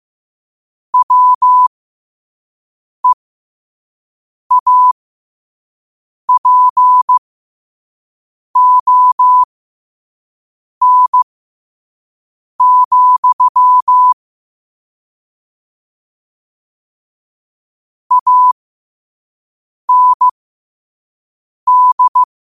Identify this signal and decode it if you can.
Morse code, heard as a keyed tone, about 15 wpm, spaced as for 5 wpm, 1000 Hz. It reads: WEAPON, AND
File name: qotdfiles / 5wpm